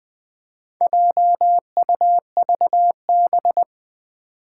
Morse code, heard as a keyed tone, about 20 wpm, 700 Hz. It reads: JUVB
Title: Four Challenging Letter-Number Combinations 20wpm